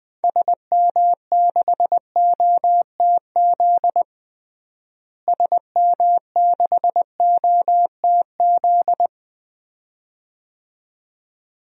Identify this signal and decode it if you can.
Morse code, heard as a keyed tone, about 20 wpm, 700 Hz. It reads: SM6OTZ SM6OTZ
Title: DXCC-100 Call Signs - 6 Characters - Repeated 2x - 20wpm